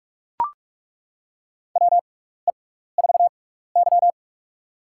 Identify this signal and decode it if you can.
Morse code, heard as a keyed tone, about 42 wpm, spaced as for 15 wpm, 700 Hz. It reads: WE4Y
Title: USA Call Signs - 6x Character Spacing - 45wpm